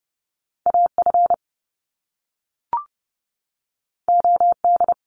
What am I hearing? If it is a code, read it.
Morse code, heard as a keyed tone, about 30 wpm, 700 Hz. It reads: AF OD
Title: Top Two Letter Combinations in Words 30wpm